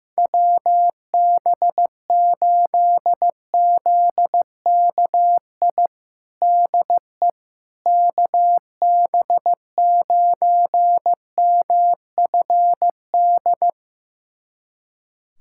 Morse code, heard as a keyed tone, about 15 wpm, 700 Hz. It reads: WB8ZKI DE KB9MFD
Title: USA Callsign DE Callsign - 6 Characters - 15wpm